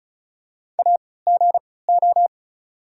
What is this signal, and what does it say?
Morse code, heard as a keyed tone, about 36 wpm, spaced as for 20 wpm, 690 Hz. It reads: AGO
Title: 3x Character Spacing - Top 500 Words - 35wpm